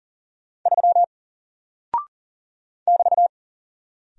Morse code, heard as a keyed tone, about 40 wpm, 700 Hz. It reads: 3 <BT>
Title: Single Character (Letter:Number:Punctuation:Prosign) Alt 40wpm